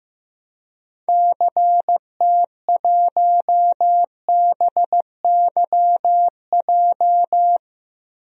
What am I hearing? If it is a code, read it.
Morse code, heard as a keyed tone, about 15 wpm, 700 Hz. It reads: CT1BYJ
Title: DXCC-100 Call Signs - 6 Characters - 15wpm